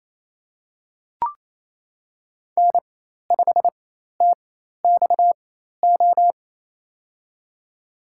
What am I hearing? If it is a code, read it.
Morse code, heard as a keyed tone, about 27 wpm, spaced as for 13 wpm, 700 Hz. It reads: N5TXO
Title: USA Call Signs - 5 Characters - 4x Character Spacing - 28wpm